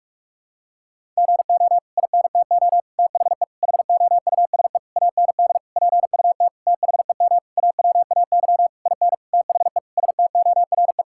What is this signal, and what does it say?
Morse code, heard as a keyed tone, about 41 wpm, 700 Hz. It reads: GO INTO THE HOUSE AND PUT THEM AWAY IN THE STORE
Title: Sentences from Top 1000 Words - Review Part 5 - 45wpm